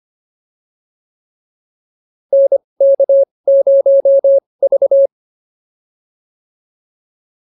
Morse code, heard as a keyed tone, about 25 wpm, spaced as for 20 wpm, 550 Hz. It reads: NK0V